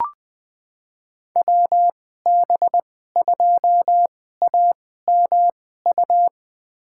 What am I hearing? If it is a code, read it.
Morse code, heard as a keyed tone, about 20 wpm, spaced as for 14 wpm, 700 Hz. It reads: WB2AMU
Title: USA Call Signs - 6 Characters - 2x Character Spacing - 20wpm